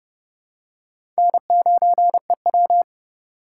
Morse code, heard as a keyed tone, about 30 wpm, 700 Hz. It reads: N9EW
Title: USA Call Signs - 4 Characters - 2x Character Spacing - 30wpm